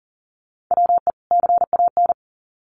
Morse code, heard as a keyed tone, about 40 wpm, 700 Hz. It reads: WE CAN
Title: Sets of 2 Words 40wpm